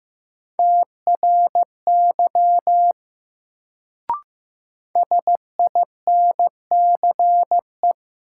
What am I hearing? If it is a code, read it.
Morse code, heard as a keyed tone, about 15 wpm, 700 Hz. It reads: TRY SINCE
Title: Sets of 2 Words - New Word Drill - 15wpm